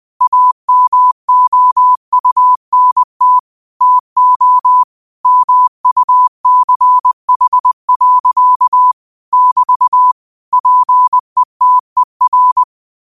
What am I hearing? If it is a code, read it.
Morse code, heard as a keyed tone, about 20 wpm, 1000 Hz. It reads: AMOUNT TO MUCH. <BT> PETER
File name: qotdfiles / 20wpm